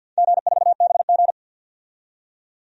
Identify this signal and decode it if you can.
Morse code, heard as a keyed tone, about 50 wpm, 700 Hz. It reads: G4BG
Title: DXCC-100 Call Signs - 4 Characters - 50wpm